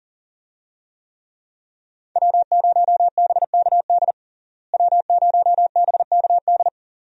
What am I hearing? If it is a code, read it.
Morse code, heard as a keyed tone, about 37 wpm, 700 Hz. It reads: W0BKD W0BKD
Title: USA Call Signs - 5 Characters - Repeated 2x - 40wpm